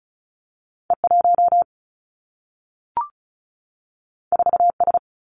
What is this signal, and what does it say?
Morse code, heard as a keyed tone, about 36 wpm, 700 Hz. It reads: E1 4S
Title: Two Challenging Letter-Number Combinations 35wpm